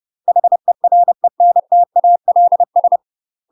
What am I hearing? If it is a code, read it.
Morse code, heard as a keyed tone, about 30 wpm, 700 Hz. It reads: HERENTALS
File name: UK MASTER QSOs TRAINING / 30WPM / UK MASTER QSOs 0014